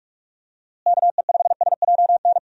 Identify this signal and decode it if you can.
Morse code, heard as a keyed tone, about 46 wpm, 700 Hz. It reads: KE5SJN
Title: USA Call Signs - 6 Characters - Repeated 3x - 45wpm